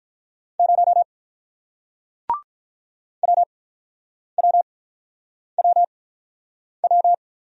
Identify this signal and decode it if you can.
Morse code, heard as a keyed tone, about 48 wpm, spaced as for 9 wpm, 700 Hz. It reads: 0 WWWW